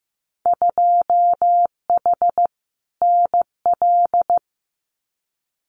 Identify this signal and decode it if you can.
Morse code, heard as a keyed tone, about 15 wpm, 700 Hz. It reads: 2H NL
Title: Winter Field Day Exchanges Encore 15wpm